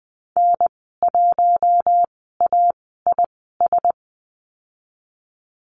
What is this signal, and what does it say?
Morse code, heard as a keyed tone, about 20 wpm, spaced as for 15 wpm, 700 Hz. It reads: N1AIS
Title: 048 Lesson 9g - Number 4 - Call Signs 20wpm